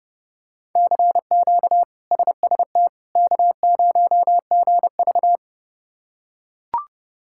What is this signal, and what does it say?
Morse code, heard as a keyed tone, about 30 wpm, 700 Hz. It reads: CQ SST K0GV
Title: K1USN SST QSO Practice - 30wpm